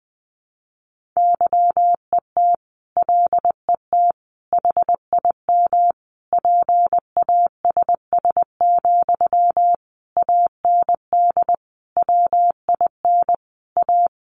Morse code, heard as a keyed tone, about 20 wpm, 700 Hz. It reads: YET LET HIM PASS, AND WIN A
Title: Sentences from Top 1500 Words 20wpm